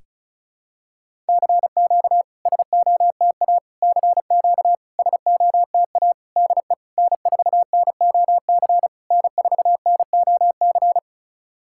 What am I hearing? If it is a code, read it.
Morse code, heard as a keyed tone, about 36 wpm, 700 Hz. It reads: CQ SOTA CQ SOTA DE N4NOC N4NOC
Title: SOTA QSO Practice 35wpm